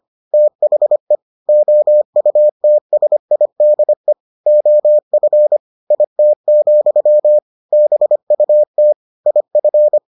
Morse code, heard as a keyed tone, about 25 wpm, 600 Hz. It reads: THE OUTSIDE OF IT, BUT IF